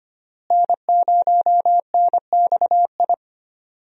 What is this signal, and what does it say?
Morse code, heard as a keyed tone, about 25 wpm, 700 Hz. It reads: N0NXI
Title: USA Call Signs - 5 Characters - Repeated 3x - 25wpm